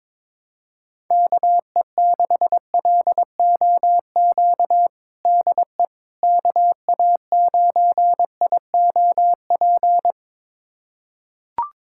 Morse code, heard as a keyed tone, about 22 wpm, 700 Hz. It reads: KE6LOQ DE KA9IOP
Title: USA Callsign DE Callsign 22wpm